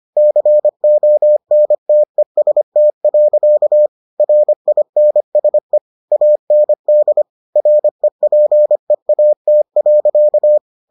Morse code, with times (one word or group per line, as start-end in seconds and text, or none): 0.2-3.9 CONTEST.
4.2-5.8 RINSE
6.1-7.2 AND
7.6-10.6 REPEAT.